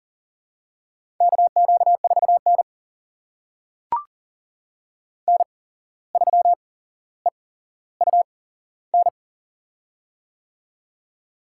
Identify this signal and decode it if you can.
Morse code, heard as a keyed tone, about 37 wpm, spaced as for 16 wpm, 700 Hz. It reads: KQ4N N3EUN
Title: USA Call Signs - 8x Character Spacing - 40wpm